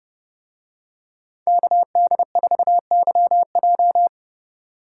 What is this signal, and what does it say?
Morse code, heard as a keyed tone, about 28 wpm, 700 Hz. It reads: KD4YJ